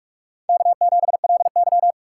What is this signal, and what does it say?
Morse code, heard as a keyed tone, about 44 wpm, 690 Hz. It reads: K7LY
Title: USA Call Signs - 4 Characters - 3x Character Spacing - 45wpm